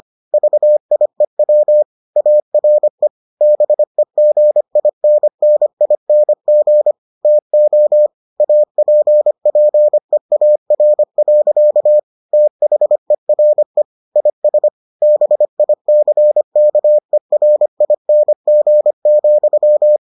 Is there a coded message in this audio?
Yes: Morse code from 0.3 to 20.0 s, heard as a keyed tone, about 25 wpm, 600 Hz. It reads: VIEW ARE BEGINNING TO APPEAR. THERE IS BICKERING,